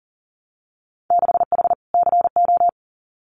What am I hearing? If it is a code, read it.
Morse code, heard as a keyed tone, about 42 wpm, 700 Hz. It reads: 6H CO
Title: Winter Field Day Exchanges Encore 40wpm